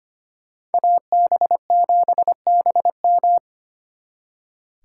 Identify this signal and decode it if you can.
Morse code, heard as a keyed tone, about 33 wpm, 700 Hz. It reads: AB7BM